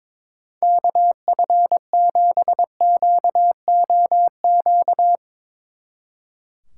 Morse code, heard as a keyed tone, about 22 wpm, 700 Hz. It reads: KF7QOQ
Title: Speed Racing - USA 2x3 Call Signs 22wpm